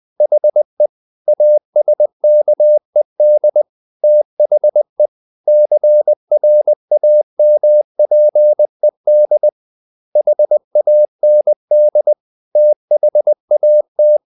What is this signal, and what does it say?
Morse code, heard as a keyed tone, about 20 wpm, 600 Hz. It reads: HE ASKED THE CRAMPED HAND THAT
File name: the old man and the sea / the old man and the sea part 1 20wpm